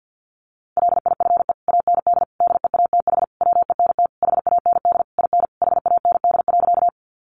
Morse code, heard as a keyed tone, about 50 wpm, 700 Hz. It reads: LIFE AND DEATH WENT HAND IN HAND.
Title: Sentences from Top 1100 Words 50wpm